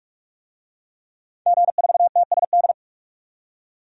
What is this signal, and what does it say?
Morse code, heard as a keyed tone, about 47 wpm, 700 Hz. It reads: G4TSD
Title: DXCC-100 Call Signs - 5 Characters - 45wpm